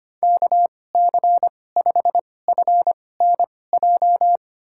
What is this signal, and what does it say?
Morse code, heard as a keyed tone, about 25 wpm, spaced as for 18 wpm, 700 Hz. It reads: KC5FNJ
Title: USA Call Signs - 6 Characters - 2x Character Spacing - 25wpm